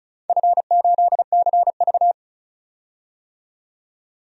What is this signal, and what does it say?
Morse code, heard as a keyed tone, about 36 wpm, 700 Hz. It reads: F8CV
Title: DXCC-100 Call Signs - 4 Characters - 35wpm